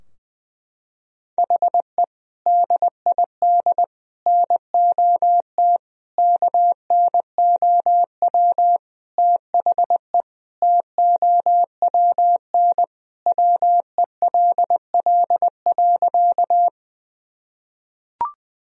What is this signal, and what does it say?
Morse code, heard as a keyed tone, about 20 wpm, 700 Hz. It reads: HE DID NOT KNOW THE TOWN WELL.
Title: Sentences from Top 1000 Words - Review Part 2 - 20wpm